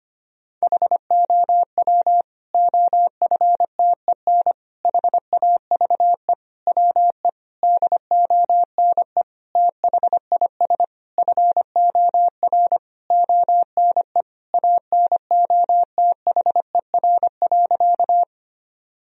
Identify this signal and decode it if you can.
Morse code, heard as a keyed tone, about 25 wpm, 700 Hz. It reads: HOW OFTEN HAVE WE DONE THIS FOR ONE ANOTHER.
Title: Sentences from Top 1000 Words - Review Part 1 - 25wpm